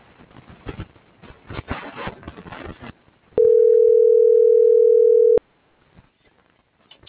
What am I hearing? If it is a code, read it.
Ringback tone (North America), 440 + 480 Hz.